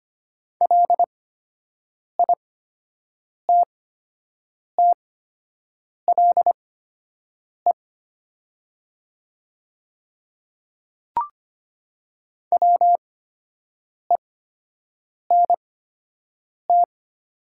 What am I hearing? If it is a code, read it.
Morse code, heard as a keyed tone, about 25 wpm, spaced as for 7 wpm, 700 Hz. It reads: LITTLE WENT
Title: 8x Character Spacing - Top 200 Words - 25wpm